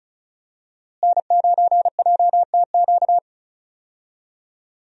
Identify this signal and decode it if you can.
Morse code, heard as a keyed tone, about 36 wpm, 700 Hz. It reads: N9JTQ
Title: USA Call Signs - 5 Characters - Repeated 3x - 35wpm